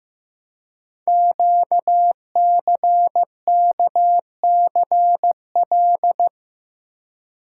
Morse code, heard as a keyed tone, about 15 wpm, 700 Hz. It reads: QCKCL
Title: Code Groups - 5 Letters - 15wpm